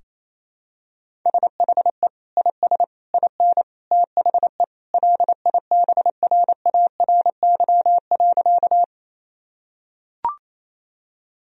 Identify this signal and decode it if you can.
Morse code, heard as a keyed tone, about 28 wpm, 700 Hz. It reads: SHE IS IN THE LIBRARY.
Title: Sentences from Top 1000 Words - Review Part 4 - 28wpm